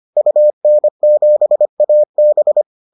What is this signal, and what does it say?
Morse code, heard as a keyed tone, about 25 wpm, 600 Hz. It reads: UN7AB